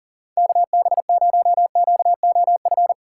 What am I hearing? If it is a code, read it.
Morse code, heard as a keyed tone, about 40 wpm, 700 Hz. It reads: KB0QOF
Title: USA Call Signs - 6 Characters - 4x Character Spacing - 40wpm